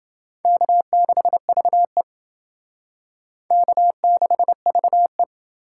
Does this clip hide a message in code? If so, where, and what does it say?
Yes: Morse code from 0.5 to 5.2 s, heard as a keyed tone, about 28 wpm, 700 Hz. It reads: K6VE K6VE